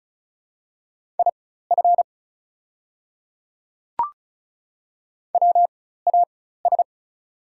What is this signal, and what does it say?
Morse code, heard as a keyed tone, about 34 wpm, spaced as for 16 wpm, 700 Hz. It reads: IF WAS
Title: 4x Character Spacing - Top 100 Words - 35wpm